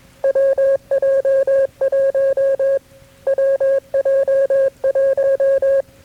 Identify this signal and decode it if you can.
Morse code, heard as a keyed tone, about 20 wpm, 540 Hz. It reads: WJ1 WJ1
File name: KB6MT-tape3-side1